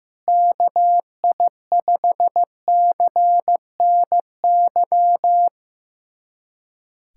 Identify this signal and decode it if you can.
Morse code, heard as a keyed tone, about 15 wpm, 700 Hz. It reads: KI5CNY
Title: USA Callsign DE Callsign - 6 Characters - 15wpm